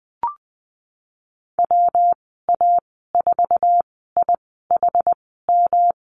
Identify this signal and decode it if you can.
Morse code, heard as a keyed tone, about 20 wpm, spaced as for 14 wpm, 700 Hz. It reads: WA4IHM
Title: 128 Lesson 21g - Number 6 - Call Signs 20wpm